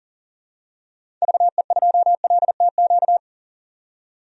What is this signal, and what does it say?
Morse code, heard as a keyed tone, about 40 wpm, 700 Hz. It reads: VE2LTQ
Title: DXCC-100 Call Signs - 6 Characters - 40wpm